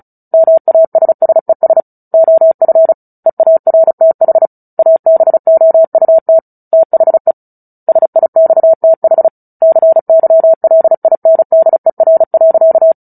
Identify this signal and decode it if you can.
Morse code, heard as a keyed tone, about 34 wpm, 650 Hz. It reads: MASSES OF EARTH ABOUT THE SIXTH CYLINDER.
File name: Book Two / 09-Chapter 9 - 35 WPM